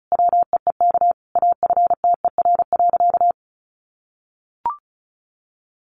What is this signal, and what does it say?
Morse code, heard as a keyed tone, about 34 wpm, 700 Hz. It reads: WEEK AFTER.
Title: Sentences from Top 500 Words 35wpm